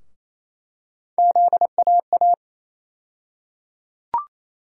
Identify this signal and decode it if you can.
Morse code, heard as a keyed tone, about 28 wpm, 700 Hz. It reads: ZAA
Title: USA Call Sign Postfixes - 3 Letters 28wpm